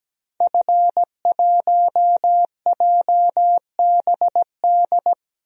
Morse code, heard as a keyed tone, about 17 wpm, 700 Hz. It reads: F1JBD
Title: DXCC-100 Call Signs - 5 Characters - Repeated 3x - 17wpm